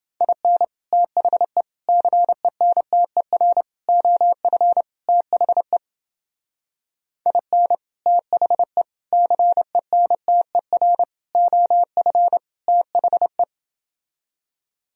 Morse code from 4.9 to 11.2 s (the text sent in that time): THE IN THE CENTER